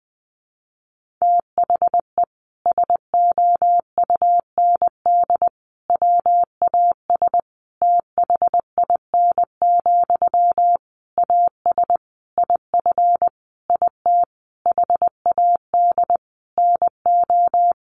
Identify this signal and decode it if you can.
Morse code, heard as a keyed tone, about 20 wpm, 700 Hz. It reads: THE SOUND WAS THIN, AS IF IT HAD NO
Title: Sentences from Top 1200 Words 20wpm